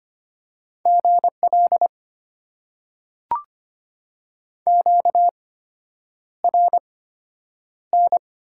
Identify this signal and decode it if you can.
Morse code, heard as a keyed tone, about 25 wpm, spaced as for 9 wpm, 700 Hz. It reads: GL QRN
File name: Top 90 QSO Elements - 8x Character Spacing - 25wpm